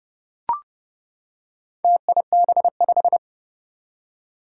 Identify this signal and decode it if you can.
Morse code, heard as a keyed tone, about 31 wpm, 700 Hz. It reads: TIB5
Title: Four Challenging Letter-Number Combinations 30wpm